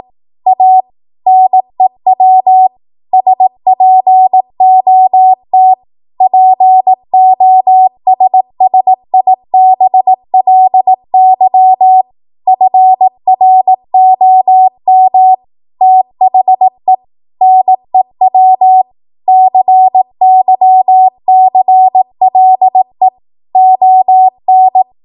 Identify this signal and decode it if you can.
Morse code, heard as a keyed tone, about 18 wpm, 750 Hz. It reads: A NEW SPOT POSSIBLY FROM THE NEW CYCLE ON
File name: Archive / 18 WPM / 18WPM